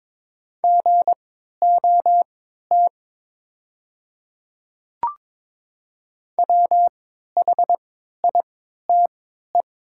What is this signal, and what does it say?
Morse code, heard as a keyed tone, about 22 wpm, spaced as for 12 wpm, 690 Hz. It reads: GOT WHITE
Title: 3x Character Spacing - Top 300 Words - 22wpm